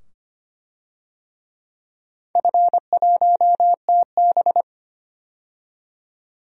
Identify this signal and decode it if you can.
Morse code, heard as a keyed tone, about 25 wpm, 700 Hz. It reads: F1TB
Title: DXCC-100 Call Signs - 4 Characters - 25wpm